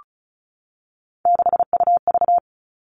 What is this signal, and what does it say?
Morse code, heard as a keyed tone, about 34 wpm, 700 Hz. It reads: 6UV
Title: All Cut Numbers - 3-Digit 35wpm